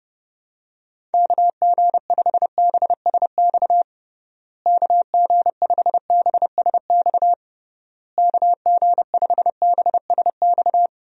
Morse code, heard as a keyed tone, about 30 wpm, 700 Hz. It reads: KG5BSX KG5BSX KG5BSX
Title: USA Call Signs - 6 Characters - Repeated 3x - 30wpm